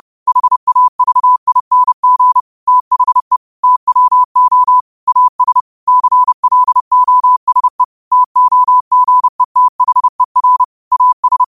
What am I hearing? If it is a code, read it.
Morse code, heard as a keyed tone, about 30 wpm, 1000 Hz. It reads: HAVING THE TWO AS CLOSE TOGETHER AS
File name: qotdfiles / 30wpm